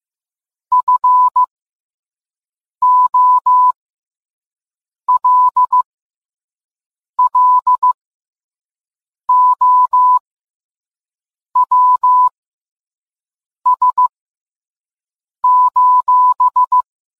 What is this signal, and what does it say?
Morse code, heard as a keyed tone, about 15 wpm, spaced as for 5 wpm, 1000 Hz. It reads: FOLLOWS:
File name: posts / gmt-games-50-off-coupon-if-you-can-read-it